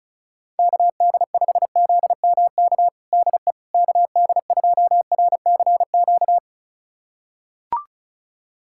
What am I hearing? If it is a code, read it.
Morse code, heard as a keyed tone, about 36 wpm, 700 Hz. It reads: KD5ZMK DE KD2RCQ
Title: USA Callsign DE Callsign - 6 Characters - 35wpm